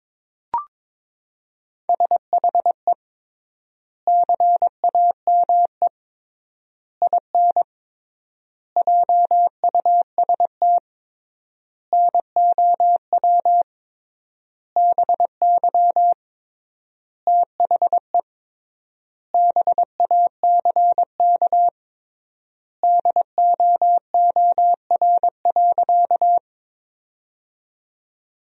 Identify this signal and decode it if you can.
Morse code, heard as a keyed tone, about 22 wpm, 700 Hz. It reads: SHE CAME IN JUST NOW BY THE BACK DOOR.
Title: 3x Word-Spacing - Sentences from Top 300 Words - 22wpm